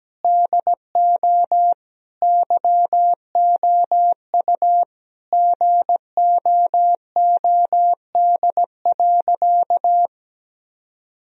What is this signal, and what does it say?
Morse code, heard as a keyed tone, about 17 wpm, 700 Hz. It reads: DO YOU GOOD.
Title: Sentences from Top 1000 Words - Review Part 1 - 17wpm